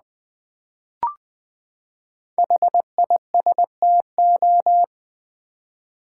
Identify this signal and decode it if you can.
Morse code, heard as a keyed tone, about 20 wpm, 700 Hz. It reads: HISTO
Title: Build A Word - 5 Letters At A Time - 10-Letter Words - 20wpm